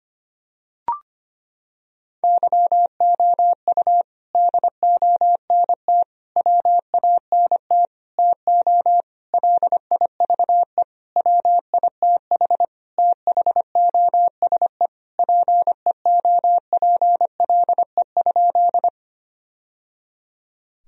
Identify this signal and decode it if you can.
Morse code, heard as a keyed tone, about 25 wpm, 700 Hz. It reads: YOU DONT WANT TO LIVE WITH THOSE PEOPLE?
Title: Sentences from Top 1000 Words - Review Part 1 - 25wpm